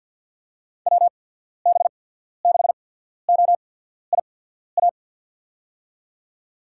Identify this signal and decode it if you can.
Morse code, heard as a keyed tone, about 53 wpm, spaced as for 14 wpm, 700 Hz. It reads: WB6OIA
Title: USA Call Signs - 6 Characters - 8x Character Spacing - 50wpm